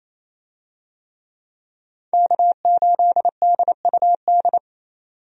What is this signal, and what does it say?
Morse code, heard as a keyed tone, about 28 wpm, 700 Hz. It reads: K8DUD
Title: USA Call Signs - 5 Characters - 28wpm